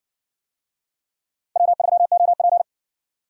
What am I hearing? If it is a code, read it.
Morse code, heard as a keyed tone, about 55 wpm, 700 Hz. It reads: W3OP